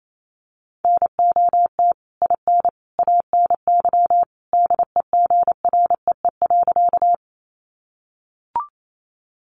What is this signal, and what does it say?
Morse code, heard as a keyed tone, about 28 wpm, 700 Hz. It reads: NOT IN ANY DEGREE.
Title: Sentences from Top 1500 Words 28wpm